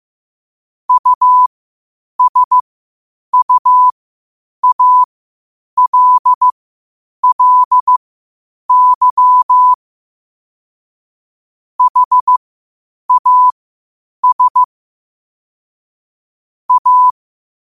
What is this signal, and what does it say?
Morse code, heard as a keyed tone, about 15 wpm, spaced as for 8 wpm, 1000 Hz. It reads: USUALLY HAS A
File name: qotdfiles / 8wpm